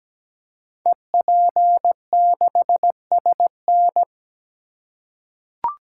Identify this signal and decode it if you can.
Morse code, heard as a keyed tone, about 17 wpm, 700 Hz. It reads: EP6SN
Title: Code Groups - 5 Letter:Numbers - 17wpm